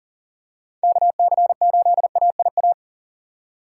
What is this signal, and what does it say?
Morse code, heard as a keyed tone, about 40 wpm, 700 Hz. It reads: KC8AIA
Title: USA Call Signs - 6 Characters - 3x Character Spacing - 40wpm